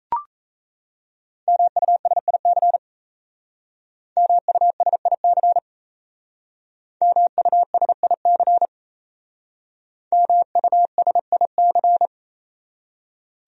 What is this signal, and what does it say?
Morse code, heard as a keyed tone, about 35 wpm, 700 Hz. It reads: MUSIC MUSIC MUSIC MUSIC
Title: Speed-Racing Top 300 Words 28wpm